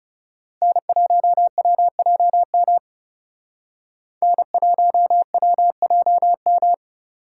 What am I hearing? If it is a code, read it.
Morse code, heard as a keyed tone, about 38 wpm, 700 Hz. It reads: N1WJM N1WJM